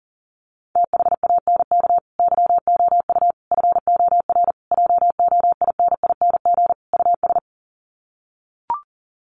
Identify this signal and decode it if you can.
Morse code, heard as a keyed tone, about 40 wpm, 700 Hz. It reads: THANK YOU FOR JOINING US
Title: Sets of 5 Words 40wpm